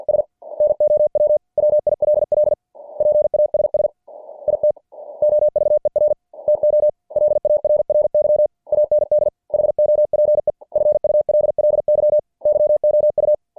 Morse code, heard as a keyed tone, about 50 wpm, 590 Hz. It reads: S NOW WILL PASS IT OVER TO LARRY AND HOPEFULLY YOU
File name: archived / CWBros